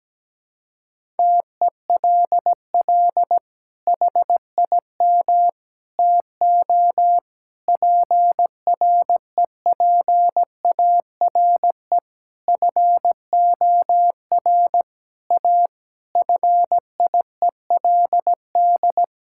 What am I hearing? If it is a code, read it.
Morse code, heard as a keyed tone, about 17 wpm, 700 Hz. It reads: TELL HIM TO PREPARE FOR A FIELD
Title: Sentences from Top 1000 Words - Review Part 6 - 17wpm